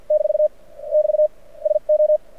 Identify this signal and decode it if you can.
Morse code, heard as a keyed tone, about 53 wpm, spaced as for 41 wpm, 590 Hz. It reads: - <BT> SO
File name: archived / CRSnet